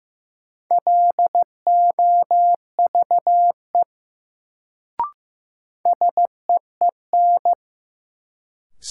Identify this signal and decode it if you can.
Morse code, heard as a keyed tone, about 15 wpm, 700 Hz. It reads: LOVE SEEN
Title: Sets of 5 Words Encore - New Word Drill - 15wpm